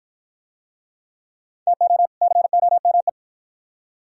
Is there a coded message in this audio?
Yes: Morse code from 1.7 to 3.1 s, heard as a keyed tone, about 53 wpm, 700 Hz. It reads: TO COME